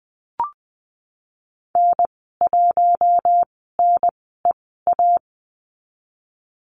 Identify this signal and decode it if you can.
Morse code, heard as a keyed tone, about 20 wpm, spaced as for 15 wpm, 700 Hz. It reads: N1NEA
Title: 055 Lesson 10g - Letter r - Call Signs 20wpm